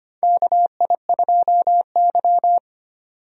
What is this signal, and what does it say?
Morse code, heard as a keyed tone, about 25 wpm, 700 Hz. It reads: KI2Y
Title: USA Call Signs - 4 Characters - 25wpm